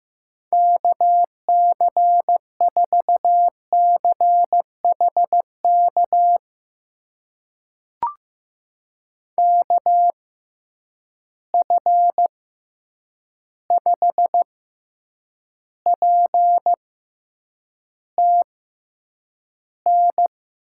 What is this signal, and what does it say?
Morse code, heard as a keyed tone, about 15 wpm, spaced as for 8 wpm, 700 Hz. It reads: KC4CHK KF5PTN
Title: USA Call Signs - 6 Characters - 6x Character Spacing - 15wpm